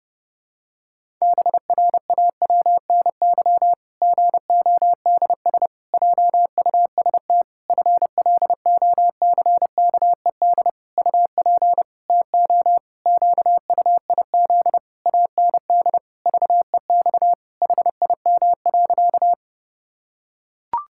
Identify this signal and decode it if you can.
Morse code, heard as a keyed tone, about 32 wpm, 700 Hz. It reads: BRAWNY GODS JUST FLOCKED UP TO QUIZ AND VEX HIM.
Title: Pangram Challenge v2 - 30wpm